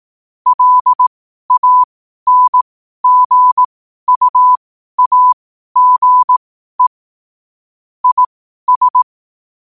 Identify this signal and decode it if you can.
Morse code, heard as a keyed tone, about 18 wpm, spaced as for 12 wpm, 1000 Hz. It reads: LANGUAGE IS
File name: qotdfiles / 12wpm